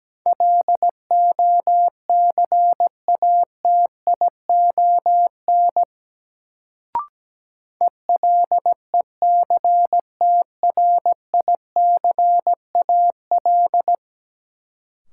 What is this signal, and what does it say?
Morse code, heard as a keyed tone, about 17 wpm, 700 Hz. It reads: LOCATION ELECTRICAL
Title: Sentences from Top 1000 Words - Review Part 6 - New Word Drill - 17wpm